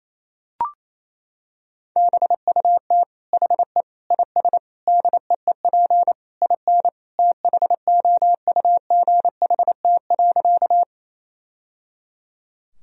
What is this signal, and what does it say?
Morse code, heard as a keyed tone, about 28 wpm, 700 Hz. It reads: BUT HE IS DEEP IN THOUGHT.
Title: Sentences from Top 1000 Words - Review Part 3 - 28wpm